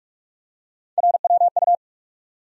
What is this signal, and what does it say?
Morse code, heard as a keyed tone, about 45 wpm, 700 Hz. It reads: RWU
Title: USA Call Sign Postfixes - 3 Letters 45wpm